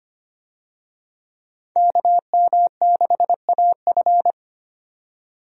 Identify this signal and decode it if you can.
Morse code, heard as a keyed tone, about 25 wpm, 700 Hz. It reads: KM6AF
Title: USA Call Signs - 5 Characters - 25wpm